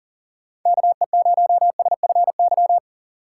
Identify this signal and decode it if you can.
Morse code, heard as a keyed tone, about 40 wpm, 700 Hz. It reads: KE0SFY
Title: USA Call Signs - 6 Characters - 4x Character Spacing - 40wpm